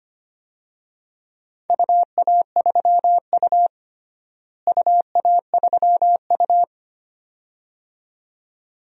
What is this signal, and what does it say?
Morse code, heard as a keyed tone, about 25 wpm, 700 Hz. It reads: UA3U UA3U
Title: DXCC-100 Call Signs - 4 Characters - Repeated 2x - 25wpm